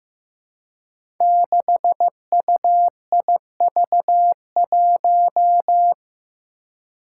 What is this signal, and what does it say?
Morse code, heard as a keyed tone, about 15 wpm, 700 Hz. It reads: 6UIV1
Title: Code Groups - 5 Letter:Numbers - 15wpm